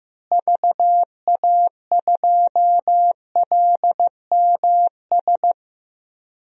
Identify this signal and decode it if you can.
Morse code, heard as a keyed tone, about 15 wpm, 700 Hz. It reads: VA2LMS
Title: DXCC-100 Call Signs - 6 Characters - Repeated 2x - 15wpm